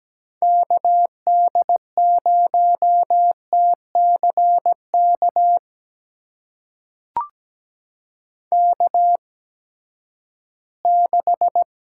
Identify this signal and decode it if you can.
Morse code, heard as a keyed tone, about 17 wpm, spaced as for 8 wpm, 700 Hz. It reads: KD0TC KK6
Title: USA Call Signs - 8x Character Spacing - 17wpm